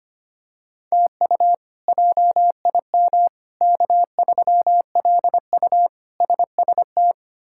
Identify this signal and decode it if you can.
Morse code, heard as a keyed tone, about 25 wpm, 700 Hz. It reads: TU JIM K3LU SST
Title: K1USN SST QSO Practice - 25wpm